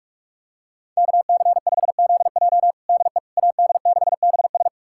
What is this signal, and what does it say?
Morse code, heard as a keyed tone, about 46 wpm, 700 Hz. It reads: KC5ZJ DE AD6BS
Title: USA Callsign DE Callsign - 5 Characters - 45wpm